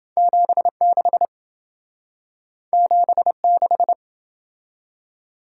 Morse code, heard as a keyed tone, about 28 wpm, 700 Hz. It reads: 76 76